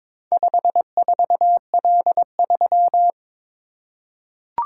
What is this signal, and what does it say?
Morse code, heard as a keyed tone, about 22 wpm, 700 Hz. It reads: 54L3
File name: practice / 22wpm / Four Letter-Number Combinations - 22wpm